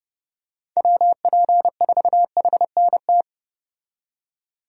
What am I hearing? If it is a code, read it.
Morse code, heard as a keyed tone, about 30 wpm, 700 Hz. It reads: WP4HNT
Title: USA Call Signs - 6 Characters - 30wpm